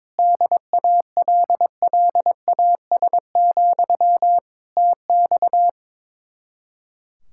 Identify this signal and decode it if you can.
Morse code, heard as a keyed tone, about 22 wpm, 700 Hz. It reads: DALLAS, TX
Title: Speed-Racing Common QSO Phrases Encore - 22wpm